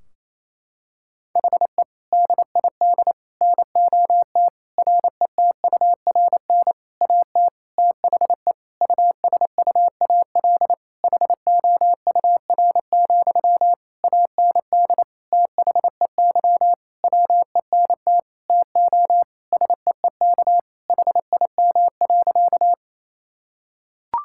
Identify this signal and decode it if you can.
Morse code, heard as a keyed tone, about 27 wpm, 700 Hz. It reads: HE DID NOT RETURN AT THE USUAL HOUR, AND THEY WENT TO SEEK HIM.
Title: Sentences from Top 1600 Words 28wpm